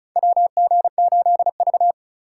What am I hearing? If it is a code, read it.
Morse code, heard as a keyed tone, about 36 wpm, 700 Hz. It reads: WG8V
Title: USA Call Signs - 4 Characters - 3x Character Spacing - 35wpm